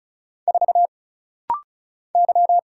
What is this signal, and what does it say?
Morse code, heard as a keyed tone, about 36 wpm, 700 Hz. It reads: 4 Y
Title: Single Letter-Number - Rapid-Fire 35wpm